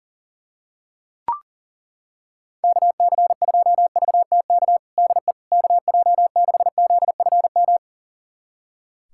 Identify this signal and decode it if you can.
Morse code, heard as a keyed tone, about 40 wpm, 700 Hz. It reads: KC2VTK DE KJ6ZFM
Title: USA Callsign DE Callsign 40wpm